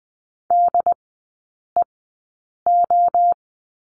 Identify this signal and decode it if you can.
Morse code, heard as a keyed tone, about 20 wpm, spaced as for 8 wpm, 700 Hz. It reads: DEO
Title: 074 Lesson 13e - Letter L - Three Characters 20wpm